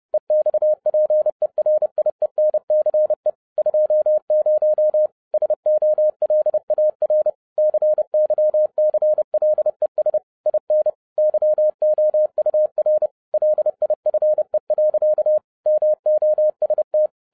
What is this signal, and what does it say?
Morse code, heard as a keyed tone, about 30 wpm, 610 Hz. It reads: EXPERIENCE 20 SOLAR CYCLES IN YOUR LIFE. MOST
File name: 2012-01-mp3 / n11-I DIDNT MISS THE SOLAR CYCLE 24-30wpm